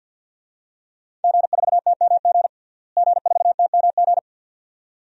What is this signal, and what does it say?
Morse code, heard as a keyed tone, about 48 wpm, 700 Hz. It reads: G4TMG G4TMG
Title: DXCC-100 Call Signs - 5 Characters - Repeated 2x - 50wpm